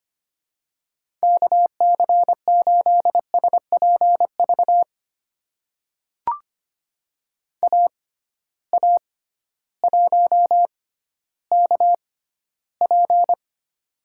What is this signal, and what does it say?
Morse code, heard as a keyed tone, about 25 wpm, spaced as for 14 wpm, 700 Hz. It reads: KC8SPV AA1KP
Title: USA Call Signs - 6x Character Spacing - 25wpm